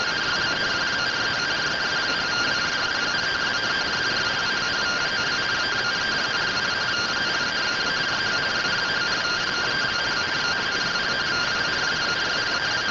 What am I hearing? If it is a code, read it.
Two-tone teleprinter signal, about 170 Hz shift.